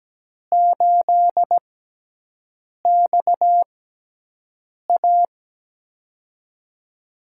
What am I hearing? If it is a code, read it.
Morse code, heard as a keyed tone, about 17 wpm, spaced as for 6 wpm, 700 Hz. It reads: 8XA
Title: USA Call Signs - 5 Characters - 6x Character Spacing - 17wpm